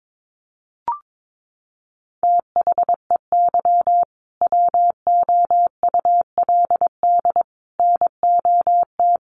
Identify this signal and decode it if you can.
Morse code, heard as a keyed tone, about 22 wpm, 700 Hz. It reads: THEY WOULD NOT
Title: Sentences from Top 1400 Words 22wpm